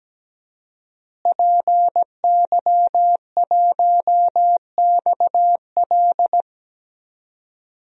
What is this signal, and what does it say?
Morse code, heard as a keyed tone, about 17 wpm, 700 Hz. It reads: PY1XL
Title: DXCC-100 Call Signs - 5 Characters - Repeated 2x - 17wpm